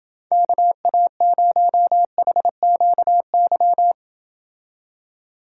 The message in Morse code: KA0HQY

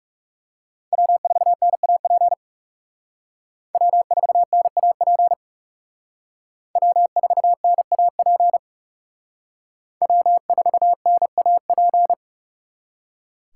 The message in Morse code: W4NAP W4NAP W4NAP W4NAP

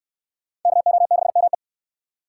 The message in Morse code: KO6ME

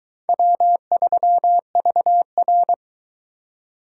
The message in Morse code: W3VR

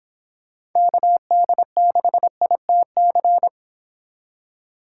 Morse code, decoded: KD6ITC

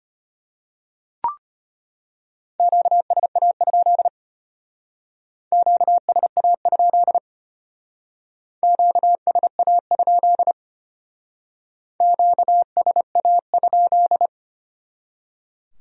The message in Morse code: QSA? QSA? QSA? QSA?

